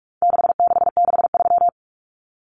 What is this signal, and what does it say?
Morse code, heard as a keyed tone, about 44 wpm, 700 Hz. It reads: 6663
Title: Numbers - 4-Digit 45wpm